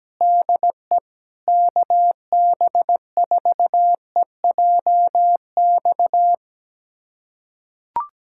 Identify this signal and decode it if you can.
Morse code, heard as a keyed tone, about 17 wpm, 700 Hz. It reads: DE KB4EJX
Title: USA Callsign DE Callsign - 6 Characters - 17wpm